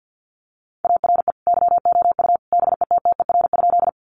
Morse code, heard as a keyed tone, about 50 wpm, 700 Hz. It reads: ARE YOU BETTER?
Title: Sentences from Top 400 Words 50wpm